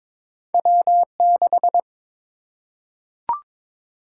Morse code, heard as a keyed tone, about 23 wpm, 700 Hz. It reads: W6
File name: practice / 22wpm / Two Letter-Number Combinations - 22wpm